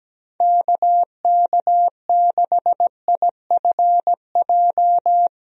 Morse code, heard as a keyed tone, about 17 wpm, 700 Hz. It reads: KK6IFJ
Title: USA Call Signs - 6 Characters - 17wpm